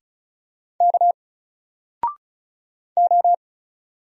Morse code, heard as a keyed tone, about 37 wpm, 700 Hz. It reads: K O
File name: Single-Letter-Number-35wpm